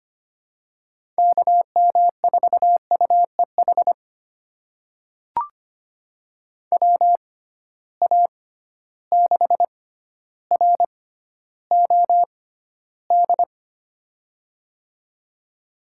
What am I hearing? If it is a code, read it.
Morse code, heard as a keyed tone, about 25 wpm, spaced as for 13 wpm, 700 Hz. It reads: KM4UEH WA6ROD
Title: USA Call Signs - 6 Characters - 6x Character Spacing - 25wpm